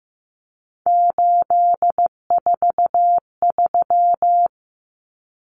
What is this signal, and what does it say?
Morse code, heard as a keyed tone, about 15 wpm, 700 Hz. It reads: 843
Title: Common Cut Numbers - 3-Digit 15wpm